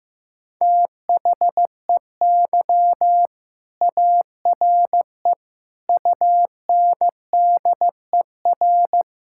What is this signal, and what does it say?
Morse code, heard as a keyed tone, about 15 wpm, 700 Hz. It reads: THEY ARE UNDER
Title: Sentences from Top 1000 Words - Review Part 6 - 15wpm